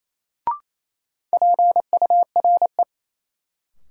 Morse code, heard as a keyed tone, about 28 wpm, 700 Hz. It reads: PURE
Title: Sentences from Top 1500 Words - New Word Drill - 28wpm